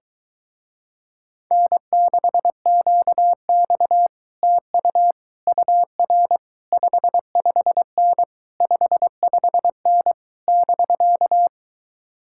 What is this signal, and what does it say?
Morse code, heard as a keyed tone, about 22 wpm, 700 Hz. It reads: N6QX TU UR 55N 55N <BK>